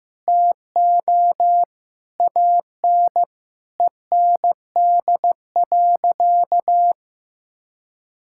Morse code, heard as a keyed tone, about 15 wpm, 700 Hz. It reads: TO AN END.
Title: Sentences from Top 1000 Words - Review Part 5 - 15wpm